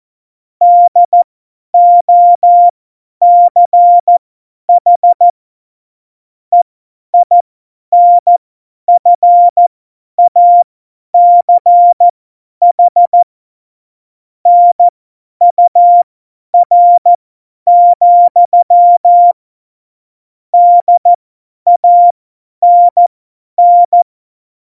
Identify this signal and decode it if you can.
Morse code, heard as a keyed tone, about 14 wpm, spaced as for 10 wpm, 700 Hz. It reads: DOCH EINFACH NUR, DANN
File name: Morsekurs / Witze / Witz 01. September 2020, 50 F70